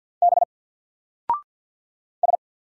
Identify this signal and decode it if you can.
Morse code, heard as a keyed tone, about 50 wpm, 700 Hz. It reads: B S
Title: Single Character (Letter:Number:Punctuation:Prosign) 50wpm